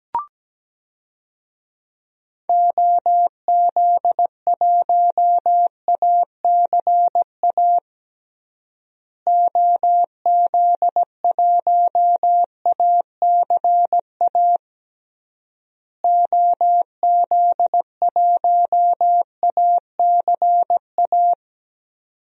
Morse code, heard as a keyed tone, about 17 wpm, 700 Hz. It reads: OZ1ACA OZ1ACA OZ1ACA
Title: DXCC-100 Call Signs - 6 Characters - Repeated 3x - 17wpm